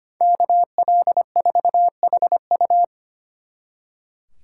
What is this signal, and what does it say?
Morse code, heard as a keyed tone, about 33 wpm, 700 Hz. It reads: KL4HU